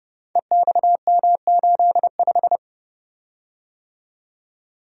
Morse code, heard as a keyed tone, about 30 wpm, 700 Hz. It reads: EXM85
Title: Code Groups - 5 Letter:Numbers - 30wpm